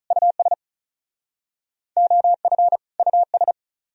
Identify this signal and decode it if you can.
Morse code, heard as a keyed tone, about 39 wpm, 700 Hz. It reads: US OF US
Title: Speed-Racing - Sets of 2 Words 35wpm